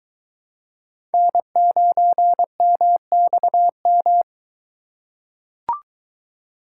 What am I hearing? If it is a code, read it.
Morse code, heard as a keyed tone, about 22 wpm, 700 Hz. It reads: N9MXM